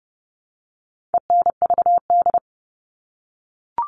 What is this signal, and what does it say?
Morse code, heard as a keyed tone, about 30 wpm, 700 Hz. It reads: ENVD
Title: All Cut Numbers - 4-Digit 30wpm